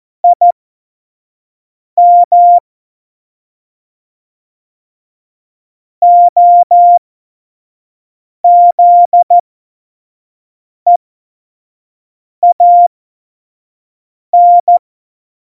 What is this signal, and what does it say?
Morse code, heard as a keyed tone, about 14 wpm, spaced as for 5 wpm, 700 Hz. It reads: IM OZEAN
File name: Morsekurs / Witze / Witz 08. August 17, Tempo 25 F70